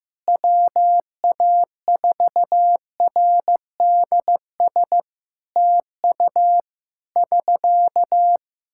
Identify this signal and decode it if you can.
Morse code, heard as a keyed tone, about 15 wpm, 700 Hz. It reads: WA4RDS TU <SK>
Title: Practice QSO - Exchange 4 - Ending - 15wpm